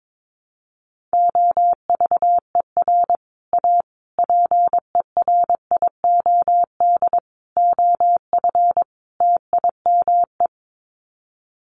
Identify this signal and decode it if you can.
Morse code, heard as a keyed tone, about 22 wpm, 700 Hz. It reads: OVER A PERIOD OF TIME
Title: Sets of 5 Words Encore2 22wpm